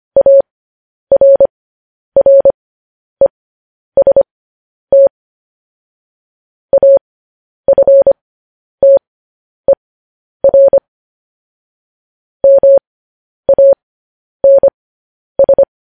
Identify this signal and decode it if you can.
Morse code, heard as a keyed tone, about 25 wpm, spaced as for 10 wpm, 550 Hz. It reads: ARREST AFTER MANS
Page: News Headlines in Morse Code at 10 WPM